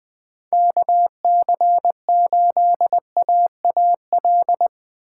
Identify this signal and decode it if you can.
Morse code, heard as a keyed tone, about 20 wpm, 700 Hz. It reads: KC8AAL
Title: USA Call Signs - 4x Character Spacing - 20wpm